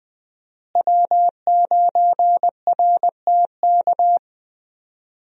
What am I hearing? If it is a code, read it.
Morse code, heard as a keyed tone, about 20 wpm, 700 Hz. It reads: W9RTK
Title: USA Call Signs - 5 Characters - Repeated 2x - 20wpm